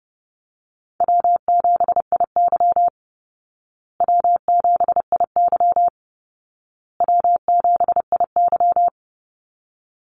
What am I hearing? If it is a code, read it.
Morse code, heard as a keyed tone, about 30 wpm, 700 Hz. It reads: W7IY W7IY W7IY
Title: Call Signs Repeated 2x - 30wpm